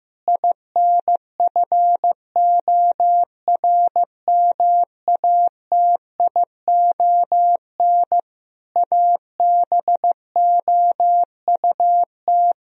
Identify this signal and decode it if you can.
Morse code, heard as a keyed tone, about 15 wpm, 700 Hz. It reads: INFORMATION ABOUT
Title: Sentences from Top 1000 Words - Review Part 3 - 15wpm